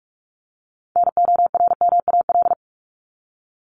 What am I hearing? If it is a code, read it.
Morse code, heard as a keyed tone, about 44 wpm, 700 Hz. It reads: NORMAL
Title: Top 1001-1250 Words 45wpm